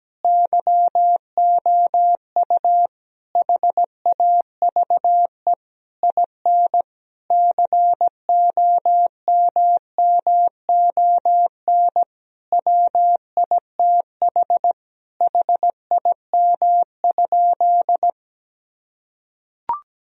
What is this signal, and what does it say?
Morse code, heard as a keyed tone, about 17 wpm, 700 Hz. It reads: YOU HAVE IN COMMON WITH HIM?
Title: Sentences from Top 1000 Words - Review Part 3 - 17wpm